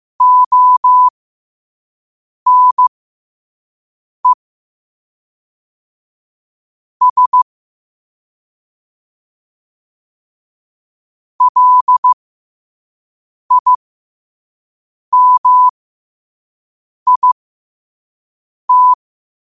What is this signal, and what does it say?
Morse code, heard as a keyed tone, about 15 wpm, spaced as for 5 wpm, 1000 Hz. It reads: ONE S LIMIT
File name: qotdfiles / 5wpm